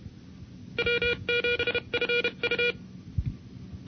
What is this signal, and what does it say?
Morse code, heard as a keyed tone, about 32 wpm, spaced as for 28 wpm, 480 Hz. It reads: W7FU
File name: NCCC sweepstakes